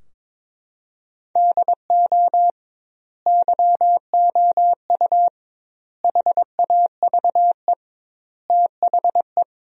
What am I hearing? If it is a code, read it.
Morse code, heard as a keyed tone, about 22 wpm, 700 Hz. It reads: DO YOU HAVE THE
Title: 2x Word-Spacing - Sentences from Top 300 Words - 22wpm